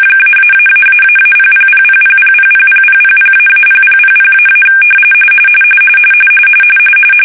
RTTY, about 45.45 baud. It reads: -------000000000000--0000--
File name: fsk-ascii850-pix / shark